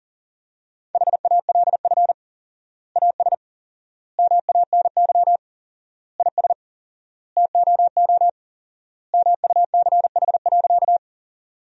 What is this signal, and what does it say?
Morse code, heard as a keyed tone, about 40 wpm, 700 Hz. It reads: HALF AS MANY IS TOO MUCH.
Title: 4x Word-Spacing - Sentences from Top 400 Words - 40wpm